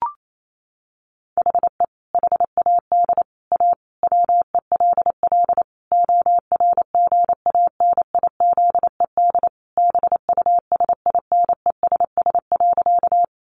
Morse code, heard as a keyed tone, about 28 wpm, 700 Hz. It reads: HE HAD A WELL ORGANIZED BUSINESS.
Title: Sentences from Top 1300 Words 28wpm